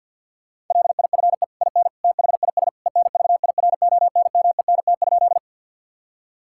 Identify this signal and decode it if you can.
Morse code, heard as a keyed tone, about 50 wpm, 700 Hz. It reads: LIFE IN THIS ENVIRONMENT?
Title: Sentences from Top 1000 Words - Review Part 5 - 50wpm